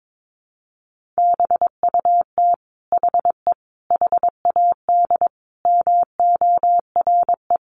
Morse code, heard as a keyed tone, about 22 wpm, 700 Hz. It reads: BUT HE HAD MORE
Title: Sentences from Top 1500 Words 22wpm